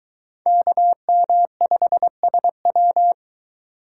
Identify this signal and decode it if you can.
Morse code, heard as a keyed tone, about 24 wpm, 700 Hz. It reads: KM5SW